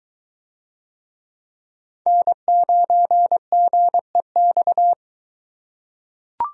Morse code, heard as a keyed tone, about 22 wpm, 700 Hz. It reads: N9GEX